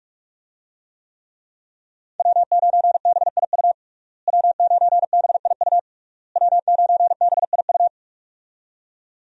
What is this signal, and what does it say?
Morse code, heard as a keyed tone, about 46 wpm, 700 Hz. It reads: W9BIU W9BIU W9BIU
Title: USA Call Signs - 5 Characters - Repeated 3x - 45wpm